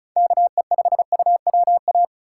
Morse code, heard as a keyed tone, about 34 wpm, 700 Hz. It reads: KE5UWA
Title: USA Call Signs - 6 Characters - 2x Character Spacing - 35wpm